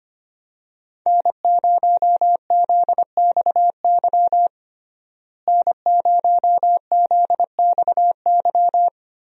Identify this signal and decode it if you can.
Morse code, heard as a keyed tone, about 25 wpm, 700 Hz. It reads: N0ZXY N0ZXY
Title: USA Call Signs - 5 Characters - Repeated 3x - 25wpm